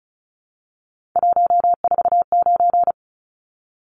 Morse code, heard as a keyed tone, about 34 wpm, 700 Hz. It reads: 149
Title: Numbers - 3-Digit 35wpm